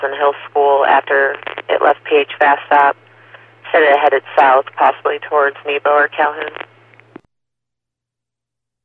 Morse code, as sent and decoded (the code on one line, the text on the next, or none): none